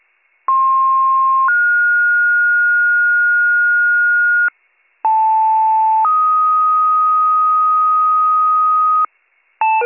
Two-tone page, heard as a one-second tone, then a three-second tone, about 1060 then 1430 Hz, about 880 then 1250 Hz, with noise. It is heard from 0.5 to 9.1 s.